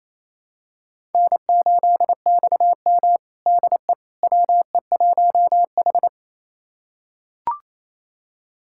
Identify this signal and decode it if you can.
Morse code, heard as a keyed tone, about 28 wpm, 700 Hz. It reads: N8XM DE WE1H
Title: USA Callsign DE Callsign - 4 Characters - 28wpm